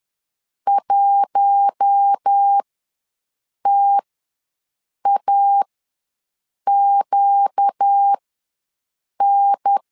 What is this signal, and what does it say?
Morse code, heard as a keyed tone, about 10 wpm, spaced as for 6 wpm, 800 Hz. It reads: 1TAQN
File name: LEZIONI / 4 Q1 ws 10wpm 2538